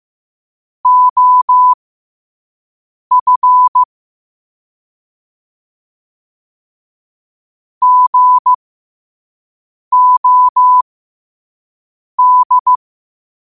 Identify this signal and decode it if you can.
Morse code, heard as a keyed tone, about 15 wpm, spaced as for 5 wpm, 1000 Hz. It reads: OF GOD
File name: qotdfiles / 5wpm